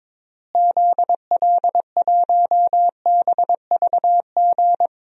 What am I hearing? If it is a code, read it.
Morse code, heard as a keyed tone, about 21 wpm, 700 Hz. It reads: ZL1BVG
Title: DXCC-100 Call Signs - 6 Characters - Repeated 3x - 22wpm